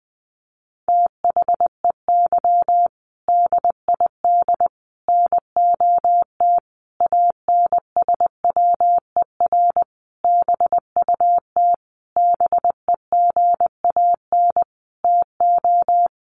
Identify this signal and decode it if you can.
Morse code, heard as a keyed tone, about 20 wpm, 700 Hz. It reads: THEY DID NOT ANSWER BUT BEGAN TO
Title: Sentences from Top 600 Words 20wpm